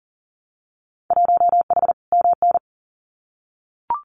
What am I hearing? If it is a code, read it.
Morse code, heard as a keyed tone, about 38 wpm, 700 Hz. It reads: 1H MN
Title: Winter Field Day Exchanges Encore 40wpm